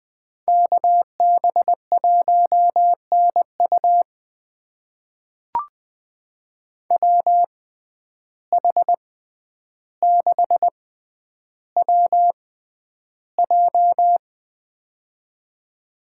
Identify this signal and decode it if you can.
Morse code, heard as a keyed tone, about 20 wpm, spaced as for 10 wpm, 700 Hz. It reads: KB1NU WH6WJ
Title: USA Call Signs - 5 Characters - 6x Character Spacing - 20wpm